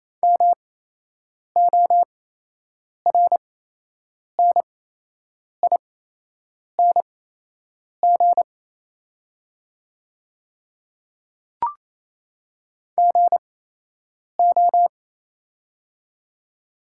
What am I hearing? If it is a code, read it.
Morse code, heard as a keyed tone, about 28 wpm, spaced as for 8 wpm, 700 Hz. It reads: MORNING GO
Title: 8x Character Spacing - Top 500 Words - 28wpm